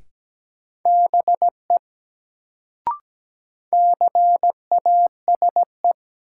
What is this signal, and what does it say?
Morse code, heard as a keyed tone, about 17 wpm, 700 Hz. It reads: BE CASE
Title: Sets of 2 Words - New Word Drill - 17wpm